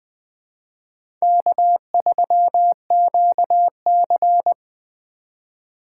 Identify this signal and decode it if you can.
Morse code, heard as a keyed tone, about 20 wpm, 700 Hz. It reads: K3QC
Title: USA Call Signs - 4 Characters - 2x Character Spacing - 20wpm